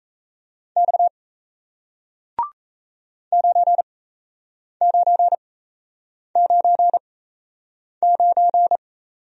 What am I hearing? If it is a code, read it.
Morse code, heard as a keyed tone, about 38 wpm, spaced as for 8 wpm, 700 Hz. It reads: X 9999